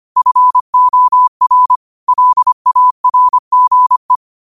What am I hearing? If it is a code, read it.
Morse code, heard as a keyed tone, about 25 wpm, 1000 Hz. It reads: FOR LARGE